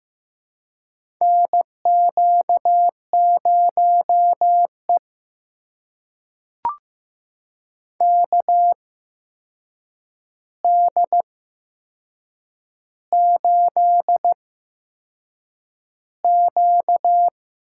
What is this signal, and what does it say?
Morse code, heard as a keyed tone, about 15 wpm, spaced as for 6 wpm, 700 Hz. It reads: NQ0E KD8Q
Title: USA Call Signs - 8x Character Spacing - 15wpm